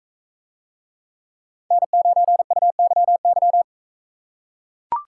Morse code, heard as a keyed tone, about 40 wpm, 700 Hz. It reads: N9UYY